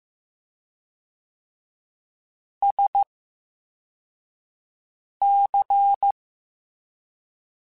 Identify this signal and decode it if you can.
Morse code, heard as a keyed tone, about 15 wpm, spaced as for 4 wpm, 800 Hz. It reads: SC
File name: Special INGs-mp3s / Special INGs-mp3s / VerySlowToFaster Ing @13